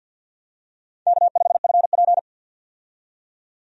Morse code, heard as a keyed tone, about 52 wpm, 700 Hz. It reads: K5FP
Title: USA Call Signs - 4 Characters - 50wpm